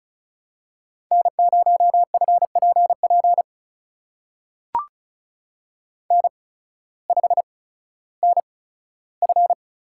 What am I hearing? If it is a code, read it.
Morse code, heard as a keyed tone, about 36 wpm, spaced as for 16 wpm, 700 Hz. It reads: N0FPP N5NF
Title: USA Call Signs - 5 Characters - 8x Character Spacing - 35wpm